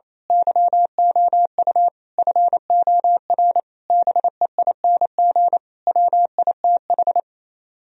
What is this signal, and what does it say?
Morse code, heard as a keyed tone, about 27 wpm, 700 Hz. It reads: YOU FOR BEING WITH
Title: Speed-Racing - Sets of 5 Words 28wpm